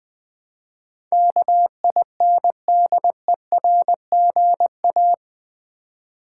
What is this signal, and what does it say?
Morse code, heard as a keyed tone, about 20 wpm, 700 Hz. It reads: KINDERGA
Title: Build A Word - 4 Letters At A Time - 12-Letter Words - 20wpm